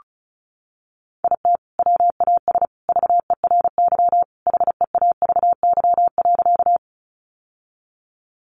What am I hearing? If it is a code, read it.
Morse code, heard as a keyed tone, about 34 wpm, 700 Hz. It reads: IT WAS VERY HEAVY.
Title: Sentences from Top 1100 Words 35wpm